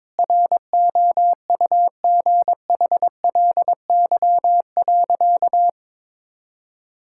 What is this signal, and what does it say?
Morse code, heard as a keyed tone, about 22 wpm, 700 Hz. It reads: ROUGHLY.
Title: Sentences from Top 1600 Words 22wpm